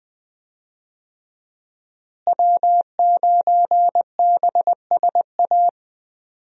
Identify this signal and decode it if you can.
Morse code, heard as a keyed tone, about 20 wpm, 700 Hz. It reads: W9BSA
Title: USA Call Signs - 5 Characters - Repeated 2x - 20wpm